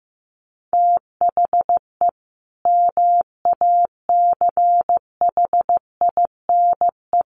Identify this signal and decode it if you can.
Morse code, heard as a keyed tone, about 15 wpm, 700 Hz. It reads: THE MACHINE
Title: A Sound Of Thunder 15wpm